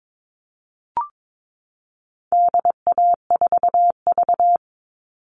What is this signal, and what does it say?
Morse code, heard as a keyed tone, about 22 wpm, 700 Hz. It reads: DA4V
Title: All Cut Numbers - 4-Digit 22wpm